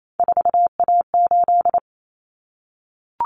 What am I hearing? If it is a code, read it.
Morse code, heard as a keyed tone, about 28 wpm, 700 Hz. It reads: 4A8
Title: Common Cut Numbers - 3-Digit 28wpm